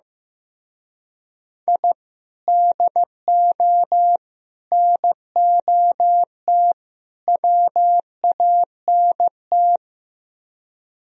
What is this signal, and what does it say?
Morse code, heard as a keyed tone, about 17 wpm, 700 Hz. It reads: I DO NOT WANT
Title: Speed-Racing - Sets of 4 Words 15wpm